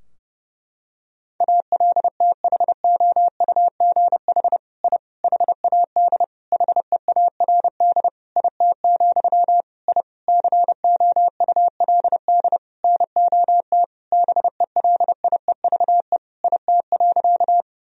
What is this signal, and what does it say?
Morse code, heard as a keyed tone, about 32 wpm, 700 Hz. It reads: ALTHOUGH I HAD HEARD IT, I COULD NOT BELIEVE IT.
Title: Sentences from Top 1000 Words - Review Part 3 - 30wpm